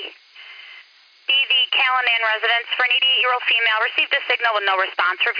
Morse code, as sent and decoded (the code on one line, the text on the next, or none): none